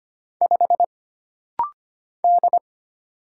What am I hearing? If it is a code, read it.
Morse code, heard as a keyed tone, about 25 wpm, 700 Hz. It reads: 5 D
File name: Single Character (Letter:Number:Punctuation:Prosign) - Rapid-Fire - 25wpm